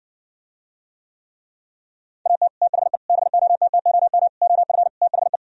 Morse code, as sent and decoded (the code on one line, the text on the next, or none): .- - / - .... / -... --- - - --- -- / --- ..-. / - .... .
AT THE BOTTOM OF THE